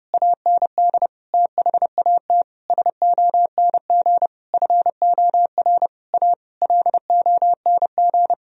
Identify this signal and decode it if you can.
Morse code, heard as a keyed tone, about 32 wpm, 700 Hz. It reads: AND THAT SONG FOR A LONG
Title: Sentences from Top 1000 Words - Review Part 2 - 30wpm